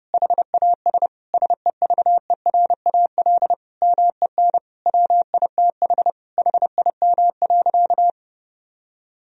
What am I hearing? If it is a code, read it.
Morse code, heard as a keyed tone, about 30 wpm, 700 Hz. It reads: HAS SEVERAL MEN WITH HIM.
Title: Sentences from Top 1000 Words - Review Part 2 - 30wpm